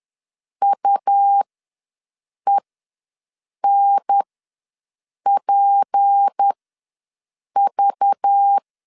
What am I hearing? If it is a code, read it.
Morse code, heard as a keyed tone, about 10 wpm, spaced as for 6 wpm, 800 Hz. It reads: UENPV
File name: LEZIONI / 6 PKX0 / 6 P ws 10wpm 3751